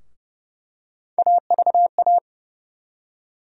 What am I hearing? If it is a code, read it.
Morse code, heard as a keyed tone, about 30 wpm, 700 Hz. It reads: AVA
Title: USA Call Sign Postfixes - 3 Letters 30wpm